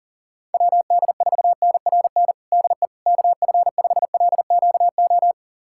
Morse code, heard as a keyed tone, about 39 wpm, 700 Hz. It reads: WD4NRN DE KF5LQO
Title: USA Callsign DE Callsign - 6 Characters - 40wpm